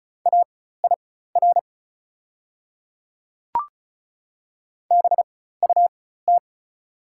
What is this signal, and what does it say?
Morse code, heard as a keyed tone, about 36 wpm, spaced as for 16 wpm, 690 Hz. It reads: AIR BUT
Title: 4x Character Spacing - Top 200 Words - 35wpm